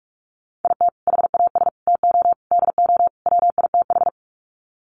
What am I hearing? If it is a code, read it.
Morse code, heard as a keyed tone, about 44 wpm, 700 Hz. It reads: IT HAS TO DO WITH
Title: Sets of 5 Words 45wpm